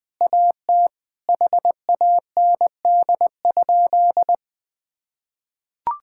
Morse code, heard as a keyed tone, about 20 wpm, 700 Hz. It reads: AT HAND?
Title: Sentences with 9-Word Phrases - 20wpm